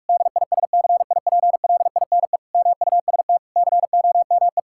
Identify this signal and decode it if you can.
Morse code, heard as a keyed tone, about 43 wpm, 700 Hz. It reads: DISCIPLINE MUST COME
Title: Sentences from Top 1000 Words - Review Part 6 - 45wpm